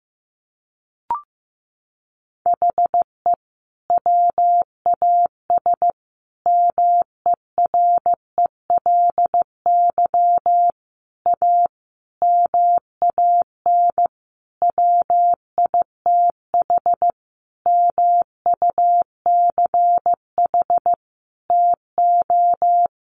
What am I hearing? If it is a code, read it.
Morse code, heard as a keyed tone, about 15 wpm, 700 Hz. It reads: HE WAS MERELY A MAN WITH MUCH TO
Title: Sentences from Top 1400 Words 15wpm